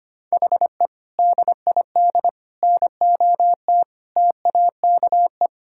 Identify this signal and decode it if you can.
Morse code, heard as a keyed tone, about 25 wpm, 700 Hz. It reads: HE DID NOT TAKE
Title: Sentences from Top 1000 Words - Review Part 6 - 25wpm